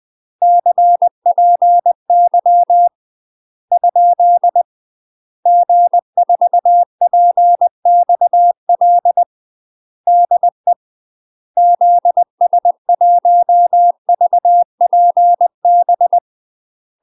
Morse code, heard as a keyed tone, about 20 wpm, 700 Hz. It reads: CPY ? G4PXL DE ZS1VPB
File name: UK MASTER QSOs TRAINING / 20WPM / UK MASTER QSOs 0036